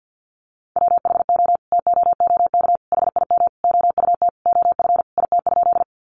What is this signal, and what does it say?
Morse code, heard as a keyed tone, about 50 wpm, 700 Hz. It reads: WHO TOOK HIM OUT OF IT?
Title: Sentences from Top 300 Words 50wpm